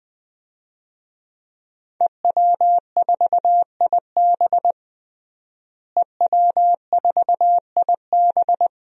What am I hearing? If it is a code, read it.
Morse code, heard as a keyed tone, about 20 wpm, 700 Hz. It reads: EW4IB EW4IB
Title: DXCC-100 Call Signs - 5 Characters - Repeated 3x - 20wpm